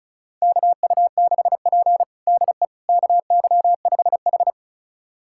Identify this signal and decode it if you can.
Morse code, heard as a keyed tone, about 36 wpm, 700 Hz. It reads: KU6P DE KY5H
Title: USA Callsign DE Callsign - 4 Characters - 35wpm